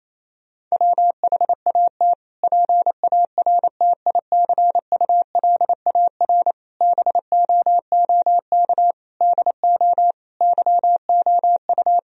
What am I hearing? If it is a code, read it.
Morse code, heard as a keyed tone, about 28 wpm, 700 Hz. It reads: WHAT PARTICULAR BOOK DO YOU
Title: Sentences from Top 1000 Words - Review Part 4 - 28wpm